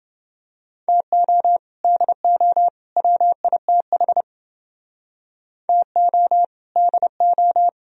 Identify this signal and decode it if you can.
Morse code, heard as a keyed tone, about 30 wpm, 700 Hz. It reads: TO DO WITH TO DO